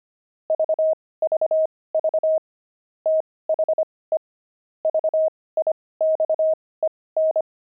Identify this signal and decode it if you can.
Morse code, heard as a keyed tone, about 25 wpm, spaced as for 18 wpm, 630 Hz. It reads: VVV THE VIXEN
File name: pangram-fs-18-cs-25-barking